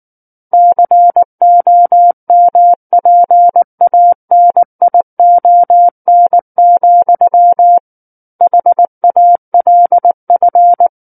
Morse code, heard as a keyed tone, about 19 wpm, 700 Hz. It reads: COMPANION, HALF
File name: Book One / 10-Chapter 10 - 19 WPM